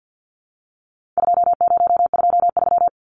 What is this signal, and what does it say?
Morse code, heard as a keyed tone, about 50 wpm, 700 Hz. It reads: 2023
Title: Numbers - 4-Digit 50wpm